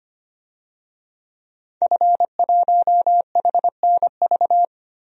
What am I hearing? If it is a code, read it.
Morse code, heard as a keyed tone, about 25 wpm, 700 Hz. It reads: F1HNV
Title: DXCC-100 Call Signs - 5 Characters - Repeated 2x - 25wpm